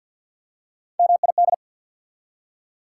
Morse code, heard as a keyed tone, about 48 wpm, 690 Hz. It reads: MID
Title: Prefixes and Suffixes 50wpm